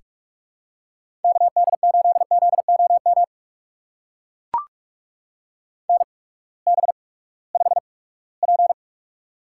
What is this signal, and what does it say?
Morse code, heard as a keyed tone, about 44 wpm, spaced as for 22 wpm, 700 Hz. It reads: KD8ZOM NB5P
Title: USA Call Signs - 8x Character Spacing - 45wpm